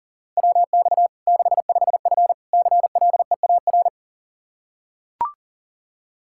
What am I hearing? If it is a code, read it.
Morse code, heard as a keyed tone, about 40 wpm, 700 Hz. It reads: WX 65F CLEAR
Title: Practice QSO - Exchange 2 - Info - 40wpm